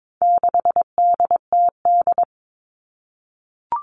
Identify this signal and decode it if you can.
Morse code, heard as a keyed tone, about 22 wpm, 700 Hz. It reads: 6DTD
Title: All Cut Numbers - 4-Digit 22wpm